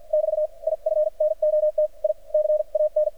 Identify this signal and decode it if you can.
Morse code, heard as a keyed tone, about 49 wpm, spaced as for 42 wpm, 610 Hz. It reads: <BT> IF NOT I CAN
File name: archived / CRSnet